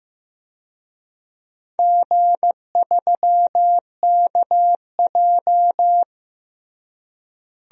Morse code, heard as a keyed tone, about 15 wpm, 700 Hz. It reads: G3KJ
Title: DXCC-100 Call Signs - 4 Characters - Repeated 2x - 15wpm